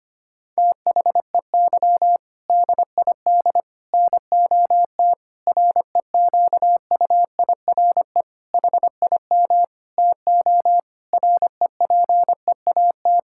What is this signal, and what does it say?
Morse code, heard as a keyed tone, about 25 wpm, 700 Hz. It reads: THEY DID NOT REQUIRE HIM TO REPEAT
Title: Sentences from Top 1600 Words 25wpm